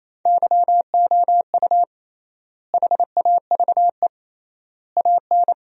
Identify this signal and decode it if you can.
Morse code, heard as a keyed tone, about 28 wpm, 700 Hz. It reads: YOU HAVE AN
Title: 3x Word-Spacing - Sentences from Top 400 Words - 28wpm